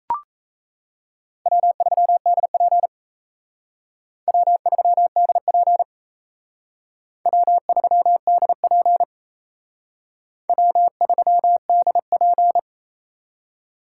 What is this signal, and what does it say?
Morse code, heard as a keyed tone, about 37 wpm, 700 Hz. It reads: W3DP W3DP W3DP W3DP